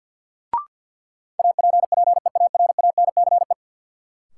Morse code, heard as a keyed tone, about 52 wpm, 700 Hz. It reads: APPEARANCE
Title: Sentences from Top 1100 Words - New Word Drill Alt - 50wpm